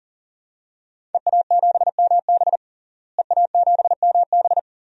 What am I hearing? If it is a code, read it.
Morse code, heard as a keyed tone, about 40 wpm, 700 Hz. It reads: EA7MB EA7MB
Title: DXCC-100 Call Signs - 5 Characters - Repeated 2x - 40wpm